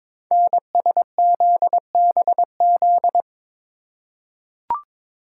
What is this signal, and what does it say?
Morse code, heard as a keyed tone, about 21 wpm, 700 Hz. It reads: NSZBZ
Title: Code Groups - 5 Letters - 22wpm